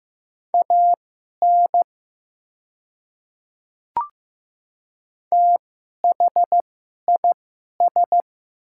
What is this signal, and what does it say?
Morse code, heard as a keyed tone, about 15 wpm, spaced as for 11 wpm, 700 Hz. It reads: AN THIS
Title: 2x Character Spacing - Top 300 Words - 15wpm